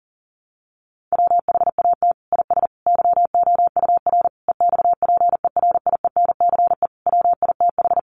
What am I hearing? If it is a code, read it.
Morse code, heard as a keyed tone, about 37 wpm, 700 Hz. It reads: WHAT IS YOUR EXPERIENCE WITH
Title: Sentences from Top 600 Words 40wpm